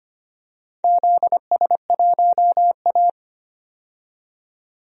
Morse code, heard as a keyed tone, about 25 wpm, 700 Hz. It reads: ZS1A
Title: DXCC-100 Call Signs - 4 Characters - 25wpm